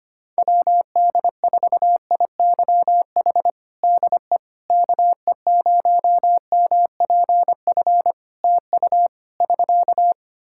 Morse code, heard as a keyed tone, about 25 wpm, 700 Hz. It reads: WD4IYH DE KE0MPF TU <SK>
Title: Practice QSO - Exchange 4 - Ending - 25wpm